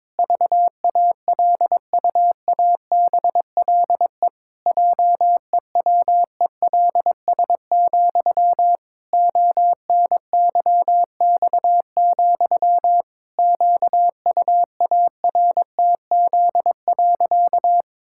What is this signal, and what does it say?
Morse code, heard as a keyed tone, about 22 wpm, 700 Hz. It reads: VALUABLE JEWELS, ONYX, QUARTZ.
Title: Pangram Challenge v2 - 22wpm